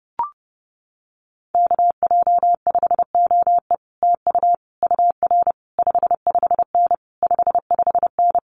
Morse code, heard as a keyed tone, about 28 wpm, 700 Hz. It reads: KJ5OE TU UR 55N 55N
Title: Common QSO Phrases Encore 30wpm